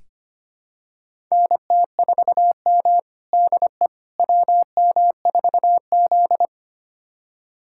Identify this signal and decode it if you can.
Morse code, heard as a keyed tone, about 25 wpm, 700 Hz. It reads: NT4M DE WM4Z
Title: USA Callsign DE Callsign - 4 Characters - 25wpm